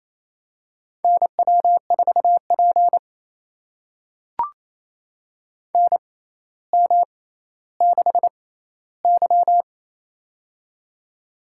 Morse code, heard as a keyed tone, about 27 wpm, spaced as for 14 wpm, 700 Hz. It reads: NW4P NM6Y
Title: USA Call Signs - 4 Characters - 6x Character Spacing - 28wpm